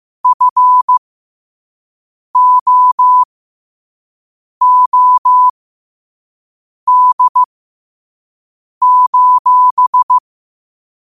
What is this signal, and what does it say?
Morse code, heard as a keyed tone, about 15 wpm, spaced as for 5 wpm, 1000 Hz. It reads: FOOD:
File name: qotdfiles / 5wpm